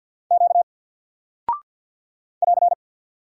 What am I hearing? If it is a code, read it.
Morse code, heard as a keyed tone, about 48 wpm, 700 Hz. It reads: Q <AR>
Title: Single Character (Letter:Number:Punctuation:Prosign) Alt 50wpm